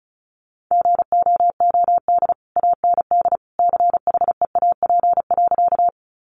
Morse code, heard as a keyed tone, about 34 wpm, 700 Hz. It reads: GOOD AND CHEAP.
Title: Sentences from Top 1200 Words 35wpm